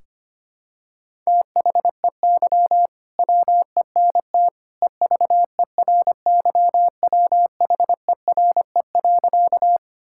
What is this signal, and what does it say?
Morse code, heard as a keyed tone, about 25 wpm, 700 Hz. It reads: THEY WENT EVERYWHERE.
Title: Sentences from Top 1600 Words 25wpm